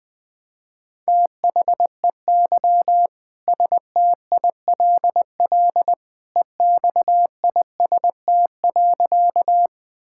Morse code, heard as a keyed tone, about 20 wpm, 700 Hz. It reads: THEY STILL EXIST.
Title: Sentences from Top 1000 Words - Review Part 5 - 20wpm